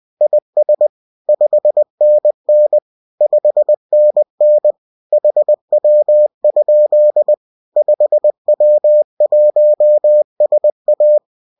Morse code, heard as a keyed tone, about 20 wpm, 600 Hz. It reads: IS 5NN 5NN HW? 5W1SA